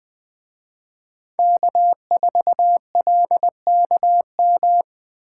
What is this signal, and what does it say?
Morse code, heard as a keyed tone, about 20 wpm, 700 Hz. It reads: K4LKM